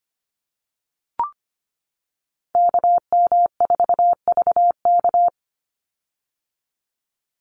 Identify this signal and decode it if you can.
Morse code, heard as a keyed tone, about 25 wpm, 690 Hz. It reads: KM4VK
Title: USA Call Signs 25wpm